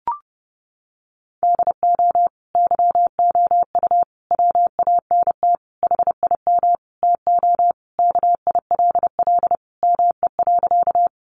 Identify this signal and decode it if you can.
Morse code, heard as a keyed tone, about 28 wpm, 700 Hz. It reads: DO YOU WANT HIM TO KILL ME.
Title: Sentences from Top 1400 Words 30wpm